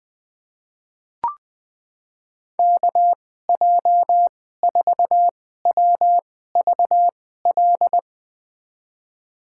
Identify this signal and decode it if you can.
Morse code, heard as a keyed tone, about 20 wpm, spaced as for 14 wpm, 700 Hz. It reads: KJ4WVL
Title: USA Call Signs - 6 Characters - 2x Character Spacing - 20wpm